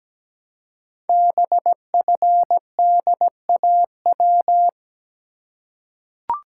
Morse code, heard as a keyed tone, about 17 wpm, 700 Hz. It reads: BFDAW
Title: Code Groups - 5 Letters - 17wpm